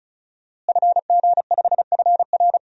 Morse code, heard as a keyed tone, about 35 wpm, 700 Hz. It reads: FG5FR
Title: DXCC-100 Call Signs - 5 Characters - Repeated 2x - 35wpm